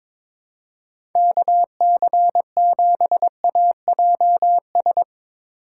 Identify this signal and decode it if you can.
Morse code, heard as a keyed tone, about 21 wpm, 700 Hz. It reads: KC7AJS
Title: USA Call Signs - 3x Character Spacing - 22wpm